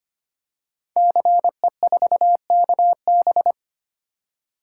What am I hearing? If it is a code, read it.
Morse code, heard as a keyed tone, about 26 wpm, 700 Hz. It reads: CE4KB
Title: Code Groups - 5 Letter:Numbers - 25wpm